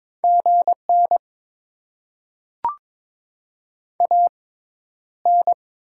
Morse code, heard as a keyed tone, about 22 wpm, spaced as for 11 wpm, 700 Hz. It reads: GN AN
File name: Top 90 QSO Elements - 6x Character Spacing - 22wpm